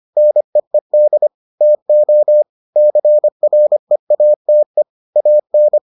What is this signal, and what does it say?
Morse code, heard as a keyed tone, about 25 wpm, 600 Hz. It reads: NEED TO CREATE AN